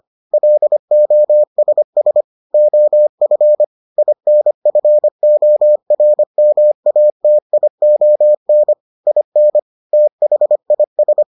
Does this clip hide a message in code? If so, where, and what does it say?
Yes: Morse code from 0.3 to 11.2 s, heard as a keyed tone, about 25 wpm, 600 Hz. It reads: LOSS OF INFORMATION IN THIS